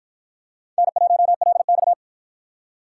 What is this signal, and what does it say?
Morse code, heard as a keyed tone, about 53 wpm, 700 Hz. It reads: N1LX